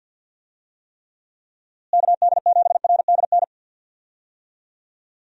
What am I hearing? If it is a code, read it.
Morse code, heard as a keyed tone, about 51 wpm, 700 Hz. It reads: KD7RDN
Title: USA Call Signs - 6 Characters - Repeated 3x - 50wpm